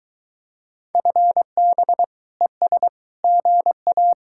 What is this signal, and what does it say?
Morse code, heard as a keyed tone, about 23 wpm, 700 Hz. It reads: FB ES GA